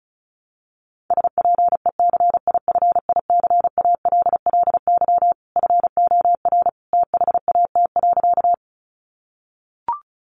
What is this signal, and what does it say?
Morse code, heard as a keyed tone, about 34 wpm, 700 Hz. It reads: SPECIFICALLY FOR THAT.
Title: Sentences from Top 1100 Words 35wpm